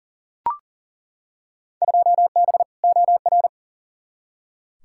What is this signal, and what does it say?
Morse code, heard as a keyed tone, about 40 wpm, 700 Hz. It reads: 2B OR
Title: Field Day Exchanges Encore - 40wpm